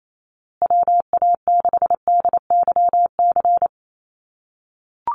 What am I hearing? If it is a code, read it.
Morse code, heard as a keyed tone, about 28 wpm, 700 Hz. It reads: WA6DYC
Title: USA Call Signs Repeated 3x 28wpm